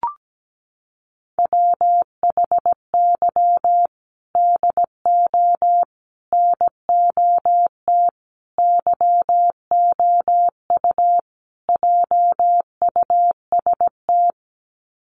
Morse code, heard as a keyed tone, about 17 wpm, 700 Hz. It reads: WHY DO NOT YOU JUST
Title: Sets of 5 Words 17wpm